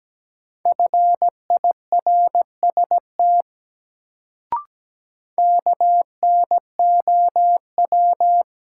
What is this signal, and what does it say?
Morse code, heard as a keyed tone, about 17 wpm, 700 Hz. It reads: FIRST KNOW
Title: Sets of 2 Words - New Word Drill - 17wpm